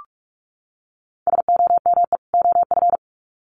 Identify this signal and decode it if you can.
Morse code, heard as a keyed tone, about 44 wpm, 700 Hz. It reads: SOME OF
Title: Sets of 2 Words 45wpm